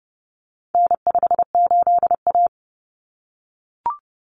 Morse code, heard as a keyed tone, about 30 wpm, 700 Hz. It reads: N58A
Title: Common Cut Numbers - 4-Digit 30wpm